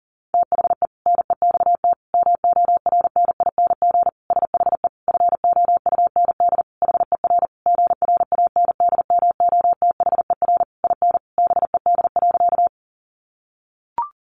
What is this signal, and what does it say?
Morse code, heard as a keyed tone, about 40 wpm, 700 Hz. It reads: THE NEXT MORNING SHE FOUND HER GRANDMOTHER IN BED.
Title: Sentences from Top 1200 Words 40wpm